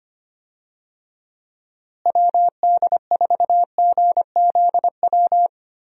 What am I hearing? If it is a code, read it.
Morse code, heard as a keyed tone, about 25 wpm, 700 Hz. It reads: WD4GZW
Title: USA Call Signs - 6 Characters - Repeated 3x - 25wpm